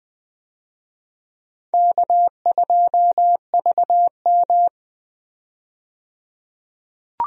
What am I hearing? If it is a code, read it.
Morse code, heard as a keyed tone, about 20 wpm, 700 Hz. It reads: K2VM
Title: USA Call Signs - 4 Characters - 20wpm